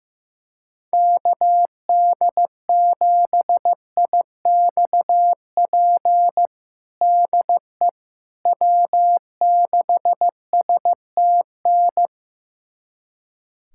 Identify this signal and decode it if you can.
Morse code, heard as a keyed tone, about 15 wpm, 700 Hz. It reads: KD7IXP DE W6STN
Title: USA Callsign DE Callsign 15wpm